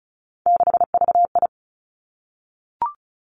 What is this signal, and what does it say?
Morse code, heard as a keyed tone, about 36 wpm, 700 Hz. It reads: 6VI
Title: Three Challenging Letter-Number Combinations 35wpm